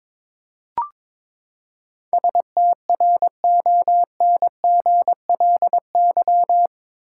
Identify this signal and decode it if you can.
Morse code, heard as a keyed tone, about 22 wpm, 700 Hz. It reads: STRONGLY
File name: practice / 22wpm / Eight Letter Words - 22wpm